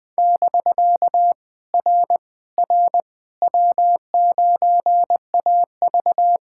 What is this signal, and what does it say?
Morse code, heard as a keyed tone, about 19 wpm, 700 Hz. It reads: <BK> R R W9AV
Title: POTA QSO Practice - 20wpm